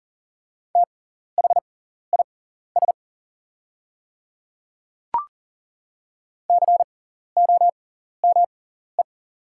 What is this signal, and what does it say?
Morse code, heard as a keyed tone, about 40 wpm, spaced as for 14 wpm, 700 Hz. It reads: THIS COME
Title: 6x Character Spacing - Top 100 Words - 40wpm